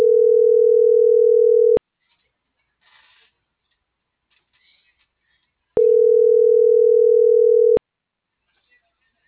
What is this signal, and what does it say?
Ringback tone (North America), 440 + 480 Hz.
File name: CENTRAL MAERSK / out isabel.bravo - .75254